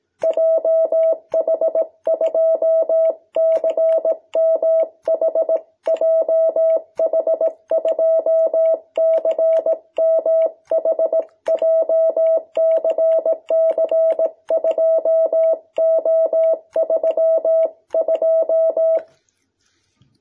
Morse code, heard as a keyed tone, about 18 wpm, 650 Hz. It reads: JH2CMH JH2CMH JCC2O32
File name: jh8cbh / 2021cw